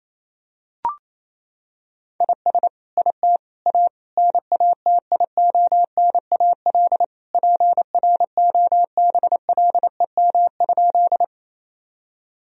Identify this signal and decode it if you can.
Morse code, heard as a keyed tone, about 28 wpm, 700 Hz. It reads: IS IT A NATIONAL PROBLEM?
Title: Sentences from Top 1000 Words - Review Part 4 - 28wpm